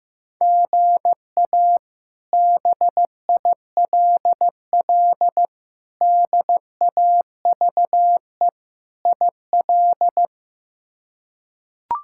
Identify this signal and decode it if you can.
Morse code, heard as a keyed tone, about 15 wpm, 700 Hz. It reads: GA BILL DAVE IL
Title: K1USN SST QSO Practice - 15wpm